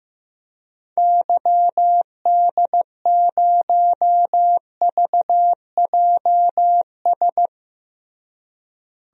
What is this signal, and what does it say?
Morse code, heard as a keyed tone, about 15 wpm, 700 Hz. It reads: YD0VJS
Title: DXCC-100 Call Signs - 6 Characters - Repeated 2x - 15wpm